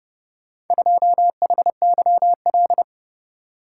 Morse code, heard as a keyed tone, about 30 wpm, 700 Hz. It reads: 2HYL